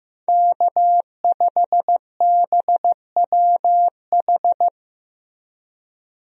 Morse code, heard as a keyed tone, about 15 wpm, 700 Hz. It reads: K5BWH
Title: USA Call Signs - 5 Characters - Repeated 2x - 15wpm